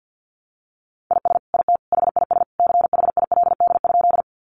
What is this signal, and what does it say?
Morse code, heard as a keyed tone, about 50 wpm, 700 Hz. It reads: IS IT HIS CHILD?
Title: Sentences from Top 800 Words 50wpm